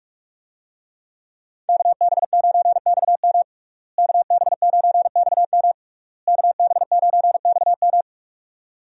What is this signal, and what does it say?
Morse code, heard as a keyed tone, about 46 wpm, 700 Hz. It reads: KB9XM KB9XM KB9XM
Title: USA Call Signs - 5 Characters - Repeated 3x - 45wpm